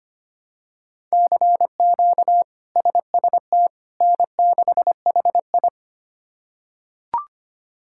Morse code, heard as a keyed tone, about 25 wpm, 700 Hz. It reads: CQ SST N6HI
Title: K1USN SST QSO Practice - 25wpm